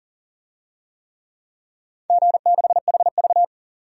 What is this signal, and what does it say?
Morse code, heard as a keyed tone, about 40 wpm, 700 Hz. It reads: G6HV
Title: DXCC-100 Call Signs - 4 Characters - 40wpm